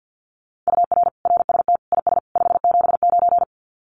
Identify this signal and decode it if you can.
Morse code, heard as a keyed tone, about 50 wpm, 700 Hz. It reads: UR RST IS 579
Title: Common QSO Phrases 50wpm